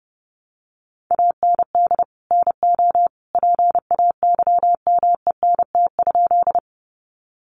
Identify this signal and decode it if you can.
Morse code, heard as a keyed tone, about 30 wpm, 700 Hz. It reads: AND NO PAYMENT?
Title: Sentences from Top 1200 Words 30wpm